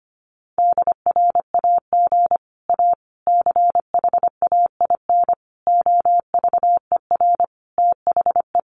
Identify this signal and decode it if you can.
Morse code, heard as a keyed tone, about 25 wpm, 700 Hz. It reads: DRAG A CHAIN OVER THE
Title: Sentences from Top 1500 Words 25wpm